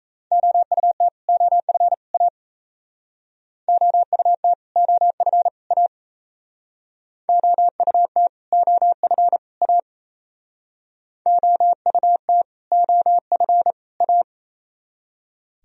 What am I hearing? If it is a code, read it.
Morse code, heard as a keyed tone, about 34 wpm, 700 Hz. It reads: OUT OF A OUT OF A OUT OF A OUT OF A
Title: Speed-Racing - Sets of 3 Words 28wpm